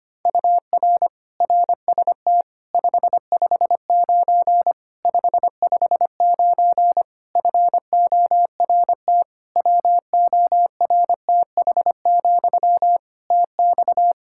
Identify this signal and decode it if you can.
Morse code, heard as a keyed tone, about 25 wpm, 700 Hz. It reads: UR RST 559 559 FORT WORTH, TX
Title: Practice QSO - Exchange 1 - Beginning - 25wpm